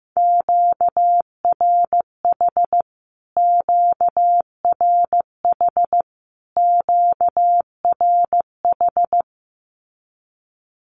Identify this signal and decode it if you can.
Morse code, heard as a keyed tone, about 15 wpm, 700 Hz. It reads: QRH QRH QRH
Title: Q-Codes Repeated 3x 15wpm